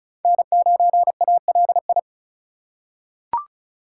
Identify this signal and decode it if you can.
Morse code, heard as a keyed tone, about 36 wpm, 700 Hz. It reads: N9ALI
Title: USA Call Signs - 5 Characters - 3x Character Spacing - 35wpm